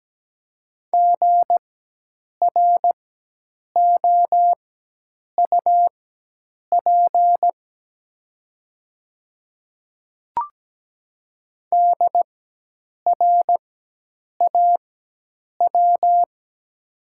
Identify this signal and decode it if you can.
Morse code, heard as a keyed tone, about 17 wpm, spaced as for 8 wpm, 690 Hz. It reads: GROUP DRAW
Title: 4x Character Spacing - Top 300 Words - 17wpm